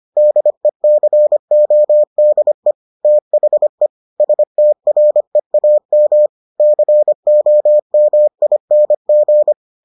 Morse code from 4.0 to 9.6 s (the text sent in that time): STREAM COMING